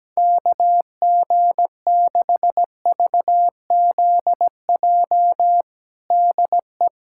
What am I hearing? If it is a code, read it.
Morse code, heard as a keyed tone, about 17 wpm, 700 Hz. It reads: KG6VZJ DE
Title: USA Callsign DE Callsign - 6 Characters - 17wpm